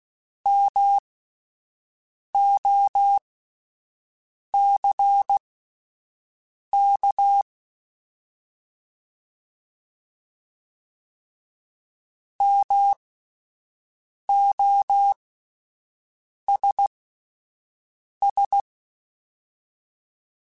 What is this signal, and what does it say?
Morse code, heard as a keyed tone, about 16 wpm, spaced as for 5 wpm, 800 Hz. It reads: MOCK MOSS
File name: Alphabetical / Alphabetical mp3s / Ms @5